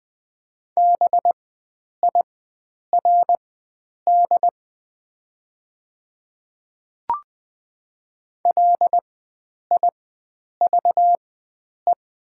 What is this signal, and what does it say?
Morse code, heard as a keyed tone, about 20 wpm, spaced as for 9 wpm, 700 Hz. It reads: BIRD LIVE
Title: 4x Character Spacing - Top 500 Words - 20wpm